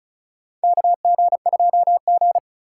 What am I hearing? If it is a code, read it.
Morse code, heard as a keyed tone, about 36 wpm, 700 Hz. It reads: KG2G
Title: USA Call Signs - 4 Characters - 4x Character Spacing - 35wpm